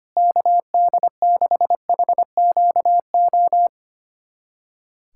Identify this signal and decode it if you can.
Morse code, heard as a keyed tone, about 25 wpm, 700 Hz. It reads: KD6HQO
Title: USA Callsign DE Callsign 25wpm